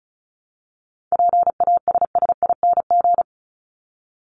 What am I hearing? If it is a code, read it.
Morse code, heard as a keyed tone, about 34 wpm, 700 Hz. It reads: PASSING
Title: ING Suffix - Top 100 Words with Base Words 35wpm